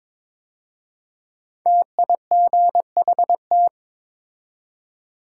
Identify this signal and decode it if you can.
Morse code, heard as a keyed tone, about 22 wpm, 700 Hz. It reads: TIGHT
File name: 5 Letter Words - First 3 Letters Given - 22wpm